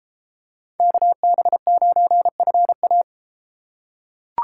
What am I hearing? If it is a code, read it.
Morse code, heard as a keyed tone, about 31 wpm, 700 Hz. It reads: KB9FA